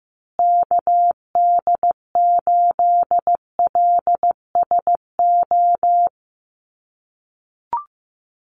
Morse code, heard as a keyed tone, about 15 wpm, 700 Hz. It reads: KD8LSO
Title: USA Call Signs Repeated 2x 15wpm